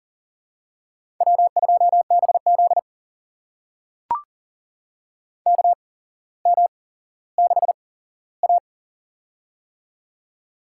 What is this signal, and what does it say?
Morse code, heard as a keyed tone, about 40 wpm, spaced as for 17 wpm, 700 Hz. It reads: W2BZ KM6A
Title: USA Call Signs - 4 Characters - 8x Character Spacing - 40wpm